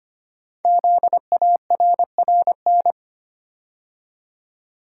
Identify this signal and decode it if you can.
Morse code, heard as a keyed tone, about 25 wpm, 700 Hz. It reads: ZARRN
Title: Code Groups - 5 Letters - 25wpm